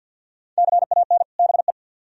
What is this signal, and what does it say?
Morse code, heard as a keyed tone, about 61 wpm, 700 Hz. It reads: CAN BE
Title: Speed-Racing - Sets of 2 Words 50wpm